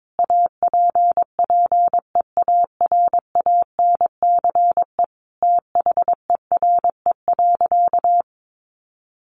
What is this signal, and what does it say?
Morse code, heard as a keyed tone, about 22 wpm, 700 Hz. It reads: APPEARANCE THERE.
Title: Sentences from Top 1100 Words 22wpm